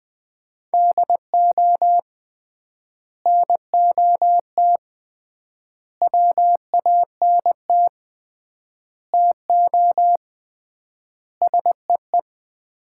Morse code, heard as a keyed tone, about 20 wpm, 700 Hz. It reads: DO NOT WANT TO SEE
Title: 3x Word-Spacing - Sets of 5 Words - 20wpm